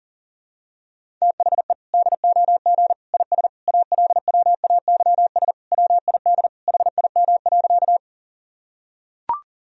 Morse code, heard as a keyed tone, about 40 wpm, 700 Hz. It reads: THE DOG IS ALWAYS WID HIM.
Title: Sentences from Top 400 Words and Abbreviations 40wpm